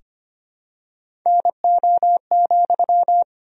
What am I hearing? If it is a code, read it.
Morse code, heard as a keyed tone, about 25 wpm, 700 Hz. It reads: NO,
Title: 4x Word-Spacing - Sentences from Top 500 Words - 25wpm